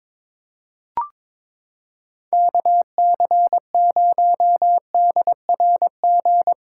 Morse code, heard as a keyed tone, about 22 wpm, 700 Hz. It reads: KC0DRG
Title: USA Callsign DE Callsign - 6 Characters - 22wpm